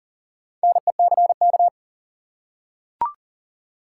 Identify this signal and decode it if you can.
Morse code, heard as a keyed tone, about 40 wpm, 700 Hz. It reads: NECK
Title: Four Letter Words - 40wpm